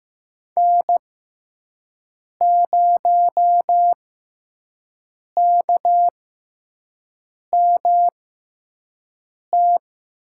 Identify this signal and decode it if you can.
Morse code, heard as a keyed tone, about 15 wpm, spaced as for 5 wpm, 700 Hz. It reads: N0KMT
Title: USA Call Signs - 5 Characters - 6x Character Spacing - 15wpm